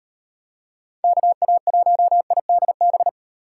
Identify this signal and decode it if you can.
Morse code, heard as a keyed tone, about 37 wpm, 700 Hz. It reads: KA1IDB